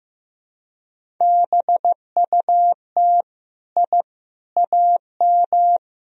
Morse code, heard as a keyed tone, about 16 wpm, 700 Hz. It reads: BUT I AM
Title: Sentences from Top 1000 Words - Review Part 2 - 15wpm